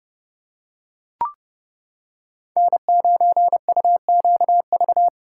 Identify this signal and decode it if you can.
Morse code, heard as a keyed tone, about 30 wpm, 700 Hz. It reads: N9UQV